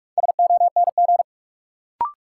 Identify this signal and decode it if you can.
Morse code, heard as a keyed tone, about 42 wpm, 700 Hz. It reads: SONG
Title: Sentences from Top 400 Words - New Word Drill - 45wpm